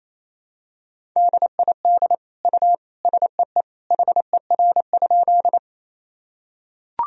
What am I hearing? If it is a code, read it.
Morse code, heard as a keyed tone, about 28 wpm, 700 Hz. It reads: DID U SEE HER?
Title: Sentences from Top 100 Words and Abbreviations 28wpm